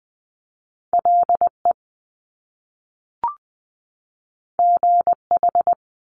Morse code, heard as a keyed tone, about 20 wpm, 700 Hz. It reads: L E G H
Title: Top Two Letter Combinations in Words 20wpm